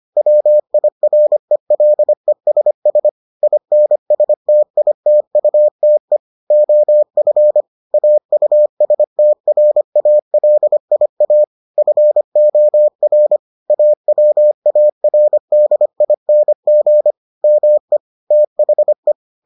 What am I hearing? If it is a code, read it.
Morse code, heard as a keyed tone, about 25 wpm, 600 Hz. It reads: WIRELESS INSTITUTE OF AUSTRALIA FOR AWARDING ME THE